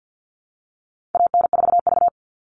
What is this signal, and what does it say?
Morse code, heard as a keyed tone, about 50 wpm, 700 Hz. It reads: AN4V
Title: All Cut Numbers - 4-Digit 50wpm